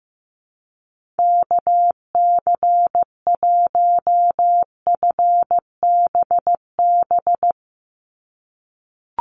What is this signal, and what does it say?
Morse code, heard as a keyed tone, about 16 wpm, 700 Hz. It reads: KC1FBB
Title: USA Call Signs 15wpm